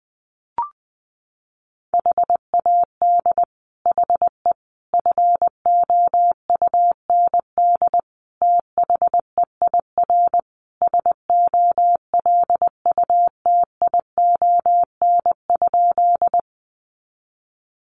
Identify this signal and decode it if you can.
Morse code, heard as a keyed tone, about 20 wpm, 700 Hz. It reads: HAD HE FOUND THEIR SOLUTION?
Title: Sentences from Top 1300 Words 20wpm